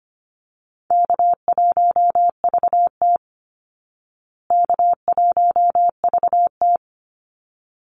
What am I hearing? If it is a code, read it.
Morse code, heard as a keyed tone, about 25 wpm, 700 Hz. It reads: K1VT K1VT
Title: Easy Call Signs Repeated 2x - 25wpm